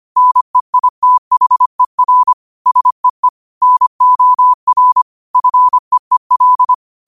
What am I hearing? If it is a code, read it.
Morse code, heard as a keyed tone, about 25 wpm, 1000 Hz. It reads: NEITHER SEE NOR FEEL
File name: qotdfiles / 25wpm